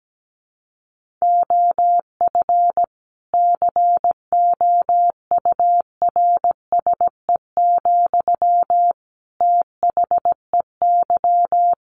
Morse code, heard as a keyed tone, about 17 wpm, 700 Hz. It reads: OF COURSE, THEY
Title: Sentences from Top 1500 Words 17wpm